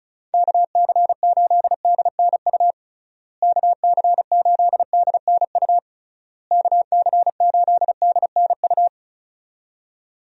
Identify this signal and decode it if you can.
Morse code, heard as a keyed tone, about 36 wpm, 700 Hz. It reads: KC8DNU KC8DNU KC8DNU
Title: USA Call Signs - 6 Characters - Repeated 3x - 35wpm